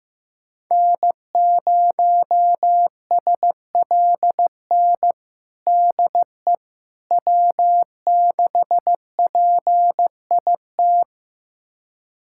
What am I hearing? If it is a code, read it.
Morse code, heard as a keyed tone, about 15 wpm, 700 Hz. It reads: N0SLN DE W6PIT
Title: USA Callsign DE Callsign - 5 Characters - 15wpm